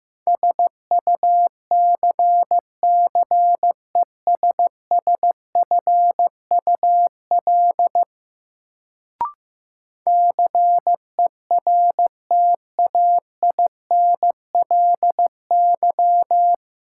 Morse code, heard as a keyed tone, about 15 wpm, 700 Hz. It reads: SUCCESSFUL CERTAINLY
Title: Sentences from Top 900 Words - New Word Drill - 15wpm